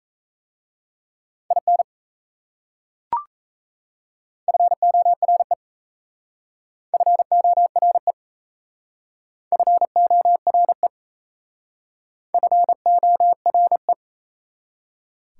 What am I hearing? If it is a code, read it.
Morse code, heard as a keyed tone, about 37 wpm, 690 Hz. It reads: IN FORE FORE FORE FORE